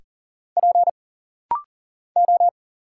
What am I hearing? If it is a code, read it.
Morse code, heard as a keyed tone, about 40 wpm, 700 Hz. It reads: P O
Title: Single Letters - Rapid-Fire 40wpm